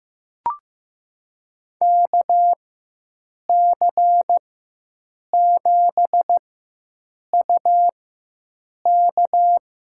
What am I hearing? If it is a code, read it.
Morse code, heard as a keyed tone, about 15 wpm, spaced as for 7 wpm, 700 Hz. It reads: KC7UK
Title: USA Call Signs - 6 Characters - 4x Character Spacing - 15wpm